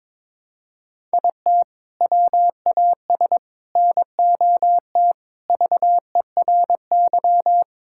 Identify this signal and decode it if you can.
Morse code, heard as a keyed tone, about 22 wpm, 700 Hz. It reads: IT WAS NOT VERY
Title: Sentences from Top 1000 Words - Review Part 5 - 22wpm